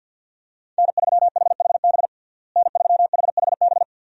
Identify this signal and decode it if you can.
Morse code, heard as a keyed tone, about 46 wpm, 700 Hz. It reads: N3HHB N3HHB
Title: USA Call Signs - 5 Characters - Repeated 2x - 50wpm